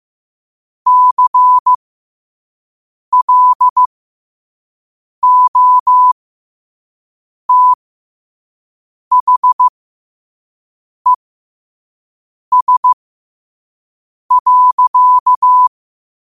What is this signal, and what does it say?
Morse code, heard as a keyed tone, about 15 wpm, spaced as for 5 wpm, 1000 Hz. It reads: CLOTHES.
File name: qotdfiles / 5wpm